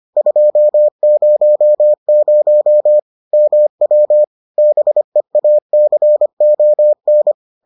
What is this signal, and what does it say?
Morse code, heard as a keyed tone, about 25 wpm, 600 Hz. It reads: 200 MW BEACON